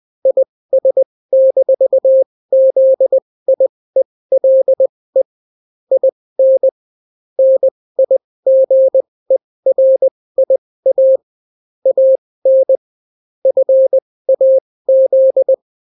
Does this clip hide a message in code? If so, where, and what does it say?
Yes: Morse code from 0.3 to 15.5 s, heard as a keyed tone, about 20 wpm, spaced as for 16 wpm, 530 Hz. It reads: IS-ZIELE IN NIGERIA AN FAZ